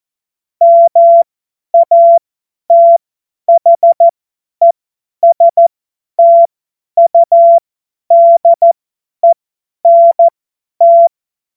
Morse code, heard as a keyed tone, about 14 wpm, spaced as for 10 wpm, 680 Hz. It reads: MATHESTUDENT
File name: Morsekurs / Witze / Witz 14. Juni 2022 50 F70